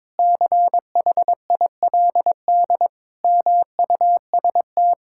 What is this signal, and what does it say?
Morse code, heard as a keyed tone, about 22 wpm, 700 Hz. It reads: CHILD MUST
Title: Sentences from Top 1000 Words - Review Part 5 - 22wpm